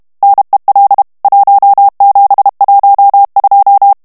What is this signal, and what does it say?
Morse code, heard as a keyed tone, about 32 wpm, 800 Hz. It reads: NEL 1712
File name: morse mp3 / ITALIEN texte clair 1 vitesse 30 mots minute